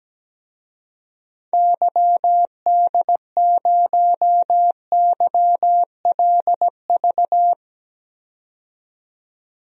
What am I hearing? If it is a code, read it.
Morse code, heard as a keyed tone, about 17 wpm, 700 Hz. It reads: YD0YLV
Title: DXCC-100 Call Signs - 6 Characters - 17wpm